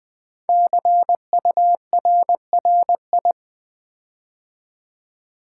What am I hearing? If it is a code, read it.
Morse code, heard as a keyed tone, about 20 wpm, 700 Hz. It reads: CURRI
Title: Build A Word - 5 Letters At A Time - 10-Letter Words - 20wpm